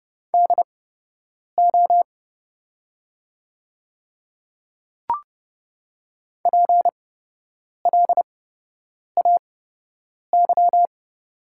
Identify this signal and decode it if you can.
Morse code, heard as a keyed tone, about 30 wpm, spaced as for 8 wpm, 700 Hz. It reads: DO PLAY
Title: 8x Character Spacing - Top 200 Words - 30wpm